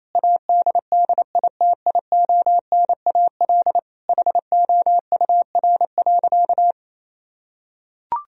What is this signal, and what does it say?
Morse code, heard as a keyed tone, about 28 wpm, 700 Hz. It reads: ADDITIONAL HOUR.
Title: Sentences from Top 1000 Words - Review Part 4 - 28wpm